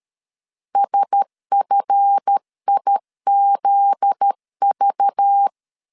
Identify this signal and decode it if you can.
Morse code, heard as a keyed tone, about 13 wpm, 800 Hz. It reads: SFIZV
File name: LEZIONI / 7 GF9Z / 7 GF9Z 12wpm 2147